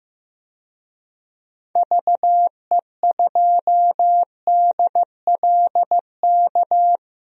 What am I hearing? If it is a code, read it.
Morse code, heard as a keyed tone, about 15 wpm, 700 Hz. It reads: VE2DLK
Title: DXCC-100 Call Signs - 6 Characters - 15wpm